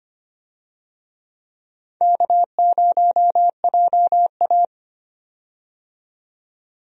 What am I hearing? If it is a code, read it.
Morse code, heard as a keyed tone, about 25 wpm, 700 Hz. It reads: K0JA
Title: USA Call Signs - 4 Characters - 25wpm